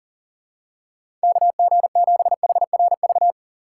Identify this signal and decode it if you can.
Morse code, heard as a keyed tone, about 40 wpm, 700 Hz. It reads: KG7HRV
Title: USA Call Signs - 4x Character Spacing - 40wpm